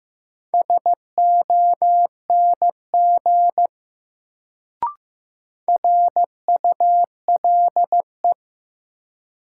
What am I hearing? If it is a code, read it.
Morse code, heard as a keyed tone, about 15 wpm, 700 Hz. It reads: SONG RULE
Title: Sentences from Top 400 Words - New Word Drill - 15wpm